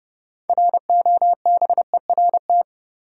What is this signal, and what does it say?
Morse code, heard as a keyed tone, about 30 wpm, 700 Hz. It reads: ROBERT
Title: USA Common Names - 30wpm